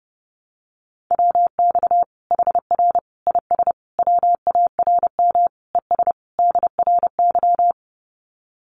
Morse code, heard as a keyed tone, about 30 wpm, 700 Hz. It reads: WX HR IS WARM ES DRY
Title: Common QSO Phrases 30wpm